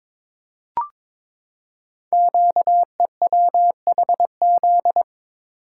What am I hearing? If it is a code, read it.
Morse code, heard as a keyed tone, about 22 wpm, 700 Hz. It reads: QEWHZ
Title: Code Groups - 5 Letter:Numbers - 22wpm